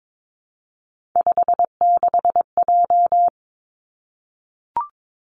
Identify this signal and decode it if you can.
Morse code, heard as a keyed tone, about 22 wpm, 700 Hz. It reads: 56J
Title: Three Challenging Letter-Number Combinations 22wpm